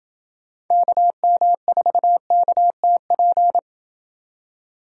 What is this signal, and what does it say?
Morse code, heard as a keyed tone, about 28 wpm, 700 Hz. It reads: KM4KTP